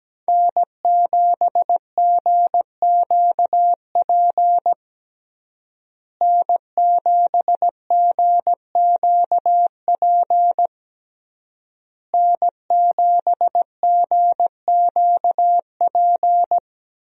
Morse code, heard as a keyed tone, about 17 wpm, 700 Hz. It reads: N7GQP N7GQP N7GQP
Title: USA Call Signs - 5 Characters - Repeated 3x - 17wpm